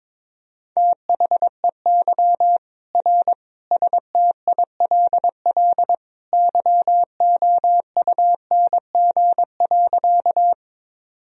Morse code, heard as a keyed tone, about 22 wpm, 700 Hz. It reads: THEY R STILL YOUNG.
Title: Sentences from Top 400 Words and Abbreviations 22wpm